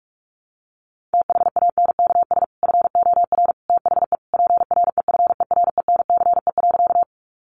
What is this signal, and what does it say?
Morse code, heard as a keyed tone, about 44 wpm, 700 Hz. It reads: THANKS FOR THE PREFERENCE.
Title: Sentences from Top 1500 Words 45wpm